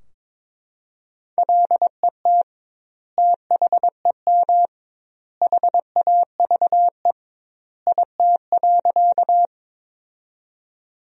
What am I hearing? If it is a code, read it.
Morse code, heard as a keyed tone, about 21 wpm, 700 Hz. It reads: LET THEM HAVE IT.
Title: 2x Word-Spacing - Sentences from Top 300 Words - 22wpm